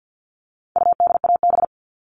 Morse code, heard as a keyed tone, about 50 wpm, 700 Hz. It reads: UDAB
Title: All Cut Numbers - 4-Digit 50wpm